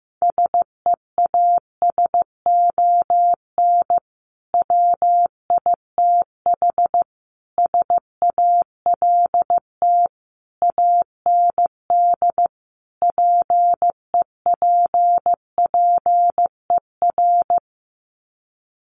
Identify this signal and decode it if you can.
Morse code, heard as a keyed tone, about 15 wpm, 700 Hz. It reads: SEASON WITH SALT AND PEPPER
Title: Sets of 5 Words 15wpm